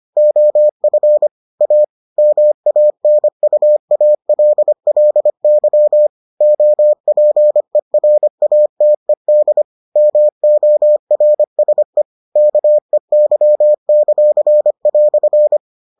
Morse code, heard as a keyed tone, about 25 wpm, 600 Hz. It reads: OF A MANUALLY OPERATED MORSE KEY;"